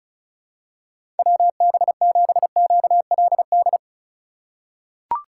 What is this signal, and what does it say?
Morse code, heard as a keyed tone, about 36 wpm, 690 Hz. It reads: WB7QLD
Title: USA Call Signs - 6 Characters - 4x Character Spacing - 35wpm